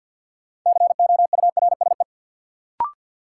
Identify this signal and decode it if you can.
Morse code, heard as a keyed tone, about 53 wpm, 700 Hz. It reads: COURSE
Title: Sentences from Top 500 Words - New Word Drill - 50wpm